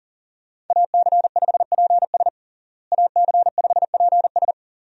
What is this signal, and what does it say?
Morse code, heard as a keyed tone, about 40 wpm, 700 Hz. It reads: AC5PS AC5PS
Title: USA Call Signs - 5 Characters - Repeated 3x - 40wpm